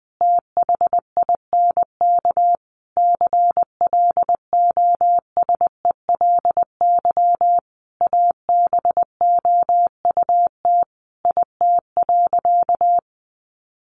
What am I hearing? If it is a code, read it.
Morse code, heard as a keyed tone, about 20 wpm, 700 Hz. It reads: THINK CLOSELY ABOUT IT.
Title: Sentences from Top 1400 Words 20wpm